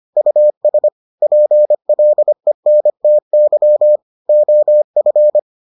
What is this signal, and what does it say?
Morse code, heard as a keyed tone, about 25 wpm, 600 Hz. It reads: US PLENTY OF